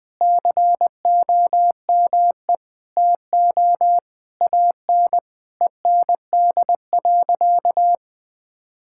Morse code, heard as a keyed tone, about 20 wpm, 700 Hz. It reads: COME TO AN END.
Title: Sentences from Top 1000 Words - Review Part 6 - 20wpm